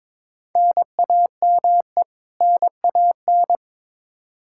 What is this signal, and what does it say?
Morse code, heard as a keyed tone, about 22 wpm, 700 Hz. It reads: NAME NAN
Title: Practice QSO - Exchange 1 - Beginning - 22wpm